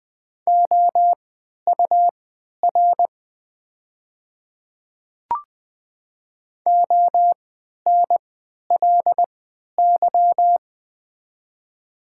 Morse code, heard as a keyed tone, about 20 wpm, spaced as for 11 wpm, 700 Hz. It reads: OUR ONLY
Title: 3x Character Spacing - Top 200 Words - 20wpm